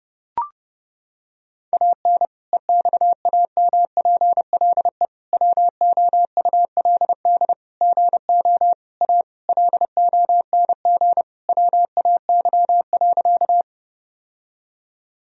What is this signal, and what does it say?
Morse code, heard as a keyed tone, about 30 wpm, 700 Hz. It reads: AN EXAMPLE WOULD GO A LONG WAY.
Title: Sentences from Top 1000 Words - Review Part 1 - 30wpm